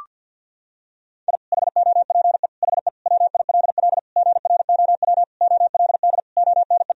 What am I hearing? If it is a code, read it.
Morse code, heard as a keyed tone, about 52 wpm, 700 Hz. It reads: I HOPE HE WILL GROW OLD ONE
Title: Sentences from Top 1000 Words - Review Part 1 - 50wpm